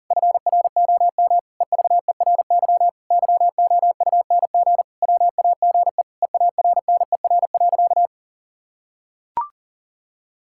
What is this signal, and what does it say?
Morse code, heard as a keyed tone, about 37 wpm, 700 Hz. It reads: FROM EVERY YOUNG WAGE EARNER.
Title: Pangram Challenge v2 - 40wpm